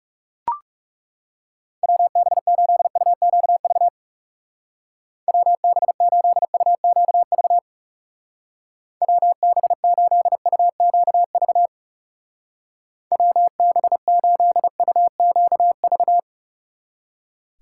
Morse code, heard as a keyed tone, about 36 wpm, 700 Hz. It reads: WB8UQV WB8UQV WB8UQV WB8UQV
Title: Speed Racing - USA 2x3 Call Signs 30wpm